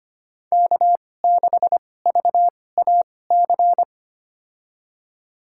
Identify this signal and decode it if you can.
Morse code, heard as a keyed tone, about 25 wpm, spaced as for 18 wpm, 700 Hz. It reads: K6VAC
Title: USA Call Signs - 5 Characters - 2x Character Spacing - 25wpm